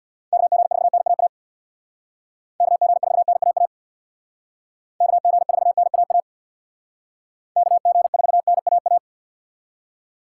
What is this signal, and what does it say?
Morse code, heard as a keyed tone, about 62 wpm, 700 Hz. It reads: KG4NAA KG4NAA KG4NAA KG4NAA
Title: Speed Racing - USA 2x3 Call Signs 50wpm